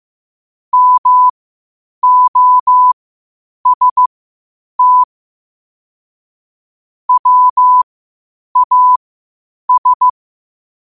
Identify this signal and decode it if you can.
Morse code, heard as a keyed tone, about 15 wpm, spaced as for 8 wpm, 1000 Hz. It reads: MOST WAS
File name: qotdfiles / 8wpm